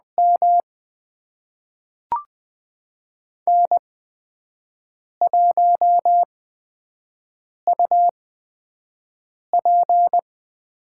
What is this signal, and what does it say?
Morse code, heard as a keyed tone, about 20 wpm, spaced as for 5 wpm, 700 Hz. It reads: M N1UP
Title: USA Call Signs - 4 Characters - 8x Character Spacing - 20wpm